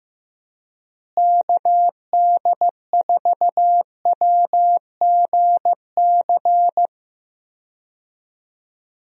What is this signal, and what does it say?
Morse code, heard as a keyed tone, about 15 wpm, 700 Hz. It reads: KD4WGC
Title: USA Call Signs - 6 Characters - Repeated 3x - 15wpm